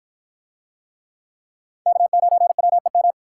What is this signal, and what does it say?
Morse code, heard as a keyed tone, about 53 wpm, 700 Hz. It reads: K9WEM